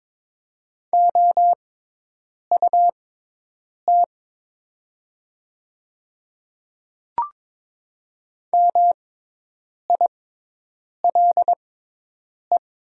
Morse code, heard as a keyed tone, about 22 wpm, spaced as for 8 wpm, 690 Hz. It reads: OUT MILE
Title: 6x Character Spacing - Top 300 Words - 22wpm